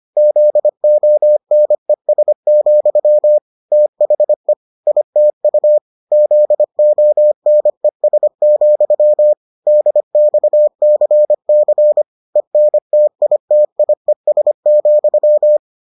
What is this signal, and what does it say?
Morse code, heard as a keyed tone, about 25 wpm, 600 Hz. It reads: ZONES, THE ITU ZONES, DXCC ENTITIES,